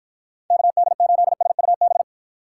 Morse code, heard as a keyed tone, about 53 wpm, 700 Hz. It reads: KD8SUB